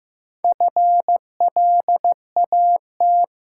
Morse code, heard as a keyed tone, about 15 wpm, 700 Hz. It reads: FLAT
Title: Four Letter Words - 15wpm